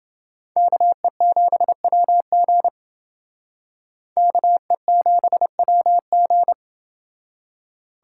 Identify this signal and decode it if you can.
Morse code, heard as a keyed tone, about 29 wpm, 700 Hz. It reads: KE7WG KE7WG